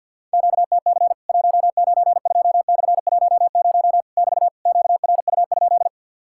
Morse code, heard as a keyed tone, about 50 wpm, 700 Hz. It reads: QTC 192/10 <BT> QRU?
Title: Worked All Europe DX Contest - QTC Exchange with Repeats - 50wpm